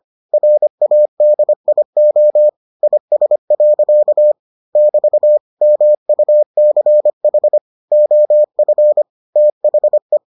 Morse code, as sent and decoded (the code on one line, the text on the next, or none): .-. .- -.. .. --- / .. ... .-.-.- / -...- / -- ..- -.-. .... / --- ..-. / - .... .
RADIO IS. <BT> MUCH OF THE